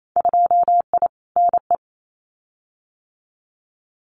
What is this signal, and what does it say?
Morse code, heard as a keyed tone, about 28 wpm, 690 Hz. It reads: 2I NE
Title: Winter Field Day Exchanges Encore 28wpm